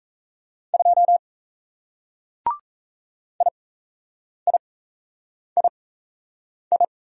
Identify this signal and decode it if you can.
Morse code, heard as a keyed tone, about 37 wpm, spaced as for 8 wpm, 700 Hz. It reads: 2 IIII